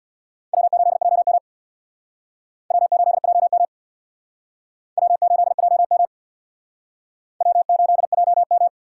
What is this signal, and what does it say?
Morse code, heard as a keyed tone, about 61 wpm, 690 Hz. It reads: W8JM W8JM W8JM W8JM
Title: Speed Racing - USA 1x2 Call Signs 50wpm